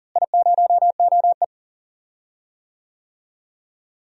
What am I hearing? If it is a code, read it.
Morse code, heard as a keyed tone, about 38 wpm, 700 Hz. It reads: I0OE
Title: DXCC-100 Call Signs - 4 Characters - Repeated 2x - 40wpm